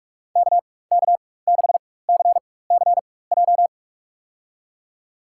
Morse code, heard as a keyed tone, about 46 wpm, spaced as for 21 wpm, 700 Hz. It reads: KK6CCJ
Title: USA Call Signs - 6 Characters - 4x Character Spacing - 45wpm